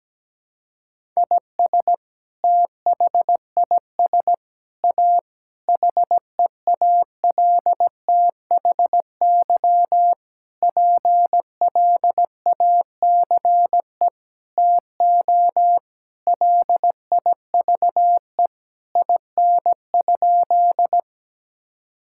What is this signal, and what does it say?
Morse code, heard as a keyed tone, about 17 wpm, 700 Hz. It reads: IS THIS A HEALTHY PLACE TO LIVE IN?
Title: Sentences from Top 1000 Words - Review Part 5 - 17wpm